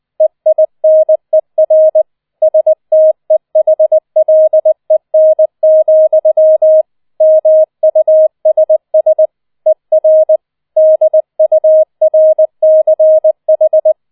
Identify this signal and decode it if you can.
Morse code, heard as a keyed tone, about 20 wpm, 630 Hz. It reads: EINER STEHLEN, MUSS ER DURCH